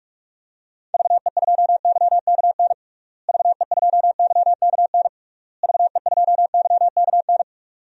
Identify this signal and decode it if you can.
Morse code, heard as a keyed tone, about 46 wpm, 700 Hz. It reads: VE2YKN VE2YKN VE2YKN
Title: DXCC-100 Call Signs - 6 Characters - Repeated 3x - 45wpm